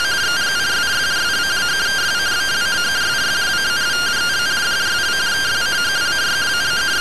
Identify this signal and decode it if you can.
Two-tone teleprinter signal, about 170 Hz shift.
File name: findings / MR CORONAL fr1 -131.gen-09 rtty rx